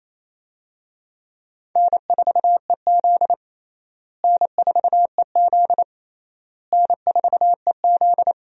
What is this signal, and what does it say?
Morse code, heard as a keyed tone, about 28 wpm, 700 Hz. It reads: N4EZ N4EZ N4EZ
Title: USA Call Signs - 4 Characters - Repeated 3x - 28wpm